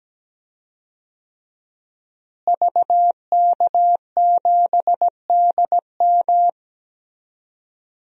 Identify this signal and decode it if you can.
Morse code, heard as a keyed tone, about 17 wpm, 700 Hz. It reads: VK7DM
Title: DXCC-100 Call Signs - 5 Characters - Repeated 2x - 17wpm